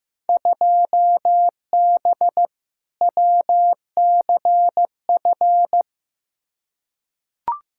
Morse code, heard as a keyed tone, about 15 wpm, 700 Hz. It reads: 2B WCF
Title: Field Day Exchanges Encore - 15wpm